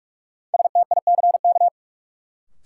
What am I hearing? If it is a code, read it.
Morse code, heard as a keyed tone, about 46 wpm, 700 Hz. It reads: STICK
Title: Sentences from Top 1200 Words - New Word Drill - 45wpm